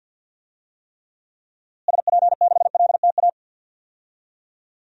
Morse code, heard as a keyed tone, about 52 wpm, 700 Hz. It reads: SP6LTA
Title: DXCC-100 Call Signs - 6 Characters - Repeated 3x - 50wpm